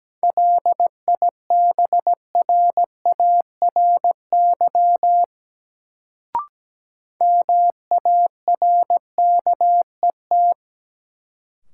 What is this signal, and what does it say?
Morse code, heard as a keyed tone, about 17 wpm, 700 Hz. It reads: LIBRARY MARKET
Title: Sentences from Top 600 Words - New Word Drill - 17wpm